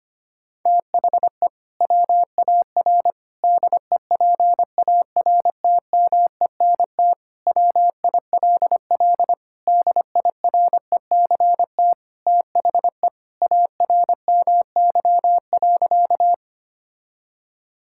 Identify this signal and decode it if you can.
Morse code, heard as a keyed tone, about 25 wpm, 700 Hz. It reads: THE WAR DEPARTMENT WILL DIRECT THE ARMY.
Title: Sentences from Top 1000 Words - Review Part 5 - 25wpm